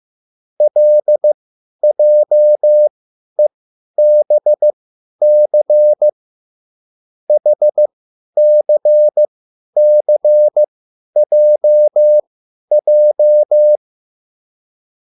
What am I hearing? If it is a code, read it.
Morse code, heard as a keyed tone, about 15 wpm, spaced as for 10 wpm, 600 Hz. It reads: LJEBC HCCJJ